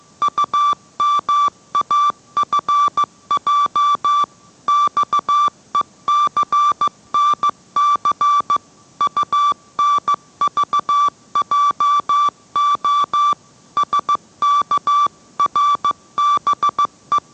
Morse code, heard as a keyed tone, about 15 wpm, 1170 Hz. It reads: UMAFJ XECNC UNVJO SKRBE